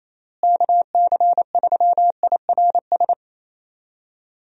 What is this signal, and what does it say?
Morse code, heard as a keyed tone, about 27 wpm, 700 Hz. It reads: KC3IRS
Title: USA Call Signs - 6 Characters - Repeated 3x - 28wpm